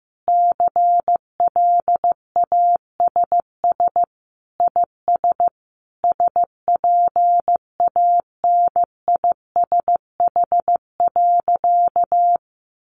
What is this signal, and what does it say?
Morse code, heard as a keyed tone, about 15 wpm, 700 Hz. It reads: CLASS IS SPANISH.
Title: Additional Easy Phrases 15wpm